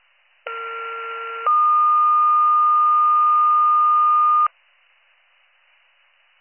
Two-tone page, heard as a one-second tone, then a three-second tone, about 1370 then 1220 Hz. It is heard from 0.5 to 4.5 s.